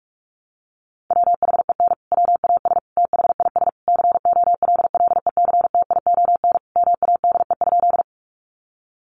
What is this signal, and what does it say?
Morse code, heard as a keyed tone, about 45 wpm, 700 Hz. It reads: WHEN WAS THIS COLLECTION MADE?
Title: Sentences from Top 1200 Words 45wpm